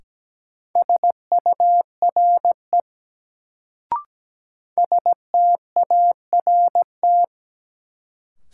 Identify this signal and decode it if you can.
Morse code, heard as a keyed tone, about 17 wpm, 700 Hz. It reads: SURE START
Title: Sentences from Top 300 Words - New Word Drill - 17wpm